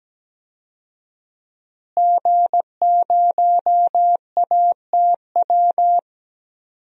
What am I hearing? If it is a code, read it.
Morse code, heard as a keyed tone, about 17 wpm, 700 Hz. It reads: G0ATW
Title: DXCC-100 Call Signs - 5 Characters - Repeated 2x - 17wpm